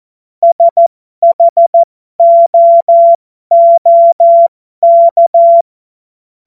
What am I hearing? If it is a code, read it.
Morse code, heard as a keyed tone, about 14 wpm, spaced as for 12 wpm, 680 Hz. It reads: SHOOK